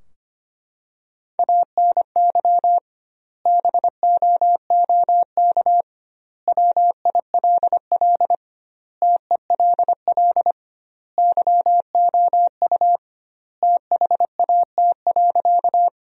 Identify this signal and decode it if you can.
Morse code, heard as a keyed tone, about 25 wpm, 700 Hz. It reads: ANY BOOK WILL TELL YOU THAT.
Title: 2x Word-Spacing - Sentences from Top 300 Words - 25wpm